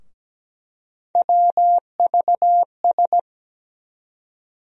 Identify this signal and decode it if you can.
Morse code, heard as a keyed tone, about 17 wpm, 700 Hz. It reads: WVS
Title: USA Call Sign Postfixes - 3 Letters 17wpm